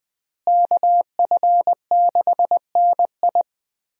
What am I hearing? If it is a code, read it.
Morse code, heard as a keyed tone, about 20 wpm, 700 Hz. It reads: KF6NI
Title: USA Call Signs - 5 Characters - Repeated 3x - 20wpm